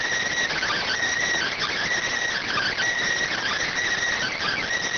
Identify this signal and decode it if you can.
Emergency Alert System (SAME) tones, heard as harsh buzzing bursts. It is heard from 0.5 to 4.6 s.